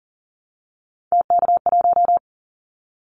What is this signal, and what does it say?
Morse code, heard as a keyed tone, about 40 wpm, 700 Hz. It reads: TK1
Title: Callsign Prefixes for Top 100 DXCC - 40wpm